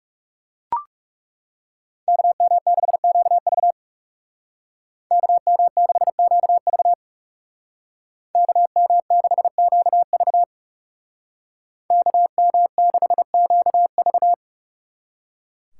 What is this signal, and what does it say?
Morse code, heard as a keyed tone, about 35 wpm, 700 Hz. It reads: KM6QV KM6QV KM6QV KM6QV
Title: Speed Racing - USA 2x2 Call Signs 30wpm